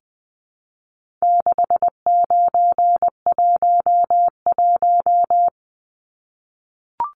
Morse code, heard as a keyed tone, about 20 wpm, 700 Hz. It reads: 6911
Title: Numbers - 4-Digit 20wpm